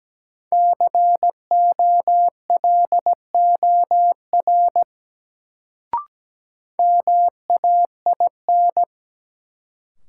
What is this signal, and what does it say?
Morse code, heard as a keyed tone, about 17 wpm, 700 Hz. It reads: COLOR MAIN
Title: Sentences from Top 1000 Words - Review Part 2 - New Word Drill - 17wpm